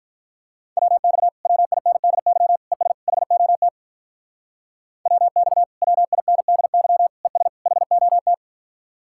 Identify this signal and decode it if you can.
Morse code, heard as a keyed tone, about 44 wpm, 700 Hz. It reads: WX WINDY ES HOT WX WINDY ES HOT